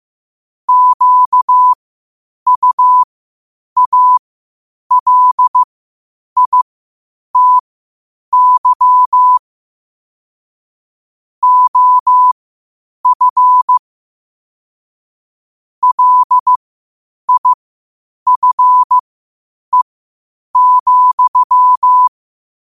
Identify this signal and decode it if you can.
Morse code, heard as a keyed tone, about 15 wpm, spaced as for 8 wpm, 1000 Hz. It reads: QUALITY OF LIFE,
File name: qotdfiles / 8wpm